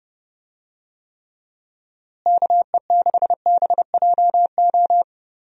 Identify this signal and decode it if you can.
Morse code, heard as a keyed tone, about 31 wpm, 700 Hz. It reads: KE6BJO
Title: USA Call Signs - 6 Characters - Repeated 3x - 30wpm